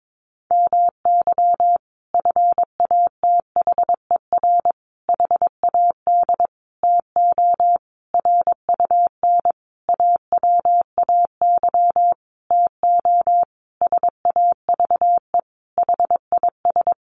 Morse code, heard as a keyed tone, about 22 wpm, 700 Hz. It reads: MY FATHER HAD TO RUN AWAY TO SAVE HIS
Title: Sentences from Top 1100 Words 22wpm